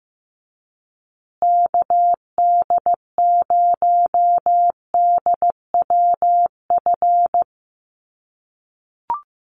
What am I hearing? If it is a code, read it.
Morse code, heard as a keyed tone, about 15 wpm, 700 Hz. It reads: KD0DWF
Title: USA Call Signs Repeated 2x 15wpm